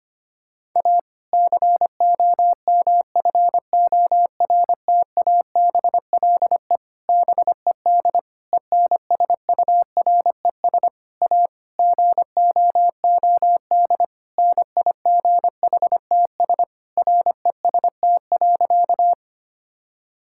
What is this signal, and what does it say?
Morse code, heard as a keyed tone, about 25 wpm, 700 Hz. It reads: A COMFORTABLE BED ENSURES A GOOD NIGHTS REST.
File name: practice / 25wpm / Difficult Sentences with Hints - 25wpm